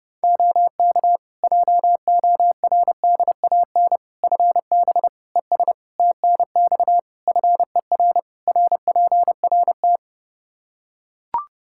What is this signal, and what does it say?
Morse code, heard as a keyed tone, about 30 wpm, 700 Hz. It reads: OK JORDAN FB ES TNX FER RPRT
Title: Practice QSO - Exchange 2 - Info - 30wpm